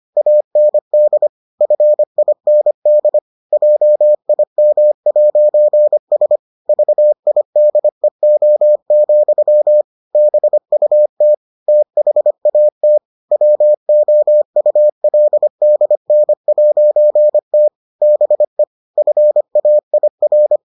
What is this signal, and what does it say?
Morse code, heard as a keyed tone, about 25 wpm, 600 Hz. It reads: AND FIND JIM'S VIDEO, BUT THAT WOULDN'T BE FAIR